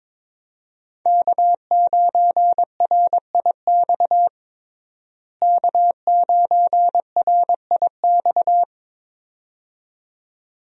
Morse code, heard as a keyed tone, about 22 wpm, 700 Hz. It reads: K9RIX K9RIX
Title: USA Call Signs - 5 Characters - Repeated 2x - 22wpm